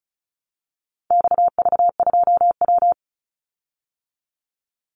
Morse code, heard as a keyed tone, about 34 wpm, 700 Hz. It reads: XV2W
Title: Call Signs Repeated 3x - 35wpm